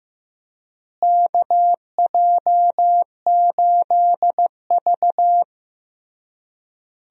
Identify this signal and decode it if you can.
Morse code, heard as a keyed tone, about 16 wpm, 700 Hz. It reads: KJ8V
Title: USA Call Signs - 4 Characters - Repeated 3x - 15wpm